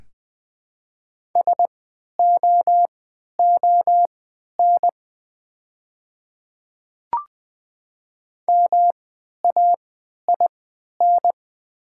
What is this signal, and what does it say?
Morse code, heard as a keyed tone, about 20 wpm, spaced as for 11 wpm, 690 Hz. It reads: SOON MAIN
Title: 3x Character Spacing - Top 500 Words - 20wpm